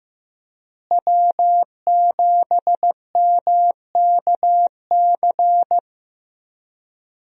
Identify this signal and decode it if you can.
Morse code, heard as a keyed tone, about 15 wpm, 700 Hz. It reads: W7MKC
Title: USA Call Signs - 5 Characters - Repeated 3x - 15wpm